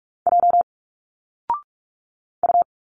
Morse code, heard as a keyed tone, about 46 wpm, 700 Hz. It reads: J U
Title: Single Letters 45wpm